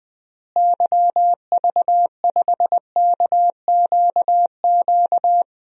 Morse code, heard as a keyed tone, about 21 wpm, 700 Hz. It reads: YV5KQQ
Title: DXCC-100 Call Signs - 6 Characters - Repeated 3x - 20wpm